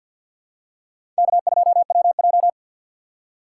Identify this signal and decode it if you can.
Morse code, heard as a keyed tone, about 47 wpm, 700 Hz. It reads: K2WJ
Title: USA Call Signs - 2x Character Spacing - 50wpm